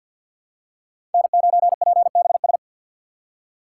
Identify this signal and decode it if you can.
Morse code, heard as a keyed tone, about 50 wpm, 700 Hz. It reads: N9PBS
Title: USA Call Signs - 3x Character Spacing - 50wpm